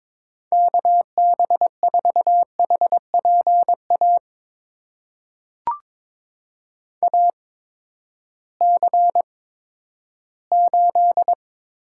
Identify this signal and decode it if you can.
Morse code, heard as a keyed tone, about 22 wpm, spaced as for 13 wpm, 700 Hz. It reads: KB4HPA AC8
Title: USA Call Signs - 8x Character Spacing - 22wpm